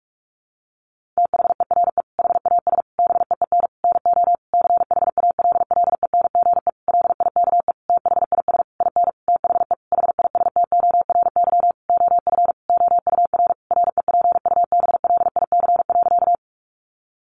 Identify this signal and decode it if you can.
Morse code, heard as a keyed tone, about 44 wpm, 700 Hz. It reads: THERE HAS BEEN NO CHALLENGE LIKE THIS IN THE HISTORY OF OUR REPUBLIC.
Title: Sentences from Top 1500 Words 45wpm